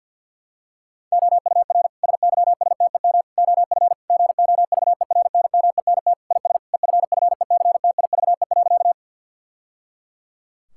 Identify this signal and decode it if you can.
Morse code, heard as a keyed tone, about 52 wpm, 700 Hz. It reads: OUR SYSTEM OF GOVERNMENT IS EFFECTIVE.
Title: Sentences from Top 1000 Words - Review Part 6 - 50wpm